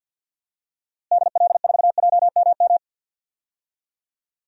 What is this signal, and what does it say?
Morse code, heard as a keyed tone, about 53 wpm, 700 Hz. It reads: DL4JMM
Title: DXCC-100 Call Signs - 6 Characters - Repeated 2x - 50wpm